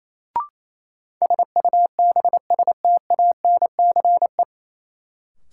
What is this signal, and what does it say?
Morse code, heard as a keyed tone, about 28 wpm, 700 Hz. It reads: SUBSTANCE
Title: Sentences from Top 1500 Words - New Word Drill - 28wpm